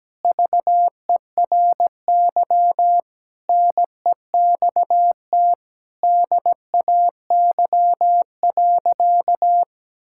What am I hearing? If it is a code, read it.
Morse code, heard as a keyed tone, about 17 wpm, 700 Hz. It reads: VERY NEXT DAY.
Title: Sentences from Top 1000 Words - Review Part 6 - 17wpm